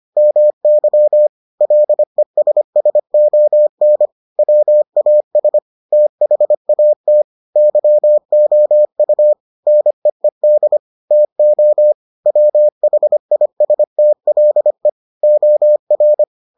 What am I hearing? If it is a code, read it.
Morse code, heard as a keyed tone, about 25 wpm, 600 Hz. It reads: MY LESSON WAS THAT YOU NEED TO WHISTLE OR